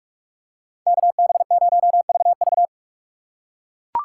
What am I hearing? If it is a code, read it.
Morse code, heard as a keyed tone, about 42 wpm, 700 Hz. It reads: KB0VV
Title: USA Call Signs - 5 Characters - 4x Character Spacing - 45wpm